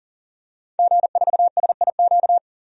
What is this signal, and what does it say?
Morse code, heard as a keyed tone, about 40 wpm, 700 Hz. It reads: G4SIQ
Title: DXCC-100 Call Signs - 5 Characters - 40wpm